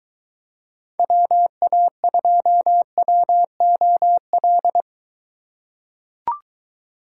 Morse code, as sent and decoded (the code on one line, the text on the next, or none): .-- .- ..--- .-- --- .-..
WA2WOL